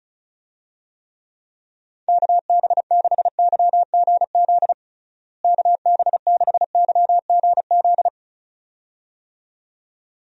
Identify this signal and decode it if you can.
Morse code, heard as a keyed tone, about 34 wpm, 700 Hz. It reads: KB6YGZ KB6YGZ
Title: USA Call Signs - 6 Characters - Repeated 2x - 35wpm